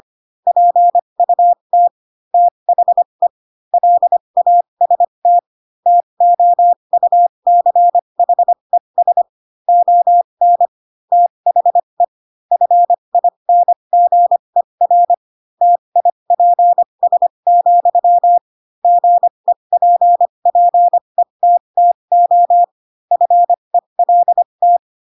Morse code, heard as a keyed tone, about 25 wpm, spaced as for 22 wpm, 700 Hz. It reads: PUT THE LAST TOUCHES ON THE FINGER TIPS, GEPPETTO FELT
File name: Pinocchio / Pinocchio ch3 18 25 ik1hge